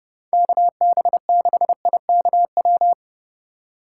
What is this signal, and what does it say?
Morse code, heard as a keyed tone, about 34 wpm, 700 Hz. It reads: KB6IKW